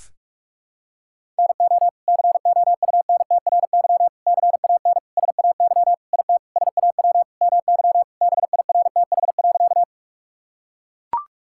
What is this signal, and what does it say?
Morse code, heard as a keyed tone, about 41 wpm, 700 Hz. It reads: NO COUNTRY CAN SAY IT SAW MY BIRTH.
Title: Sentences from Top 1000 Words - Review Part 5 - 45wpm